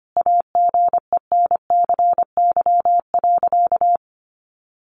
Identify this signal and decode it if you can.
Morse code, heard as a keyed tone, about 25 wpm, 700 Hz. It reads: AGENCY.
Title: Sentences from Top 1300 Words 25wpm